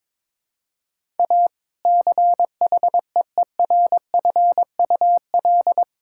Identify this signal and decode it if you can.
Morse code, heard as a keyed tone, about 22 wpm, 700 Hz. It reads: A CHEERFUL
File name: Difficult Sentences with Hints - 22wpm